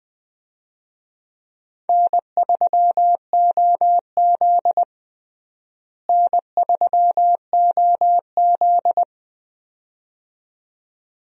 Morse code, heard as a keyed tone, about 21 wpm, 700 Hz. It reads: N3OZ N3OZ
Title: USA Call Signs - 4 Characters - Repeated 2x - 20wpm